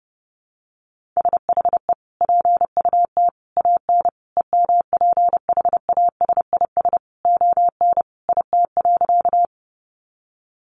Morse code, heard as a keyed tone, about 30 wpm, 700 Hz. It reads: SHE PUT AN EMPHASIS ON IT.
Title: Sentences from Top 1400 Words 30wpm